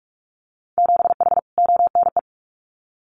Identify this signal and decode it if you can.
Morse code, heard as a keyed tone, about 46 wpm, 700 Hz. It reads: 7H ONE
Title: Winter Field Day Exchanges Encore 45wpm